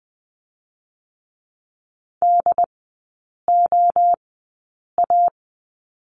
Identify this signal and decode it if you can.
Morse code, heard as a keyed tone, about 20 wpm, spaced as for 8 wpm, 700 Hz. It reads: DOA
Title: 067 Lesson 12e - Letter d - Three Characters 20wpm